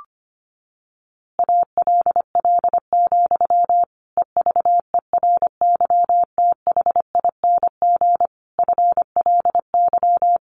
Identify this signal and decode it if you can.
Morse code, heard as a keyed tone, about 25 wpm, 700 Hz. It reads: ALL, EVERYTHING FLY
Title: A Sound Of Thunder 25wpm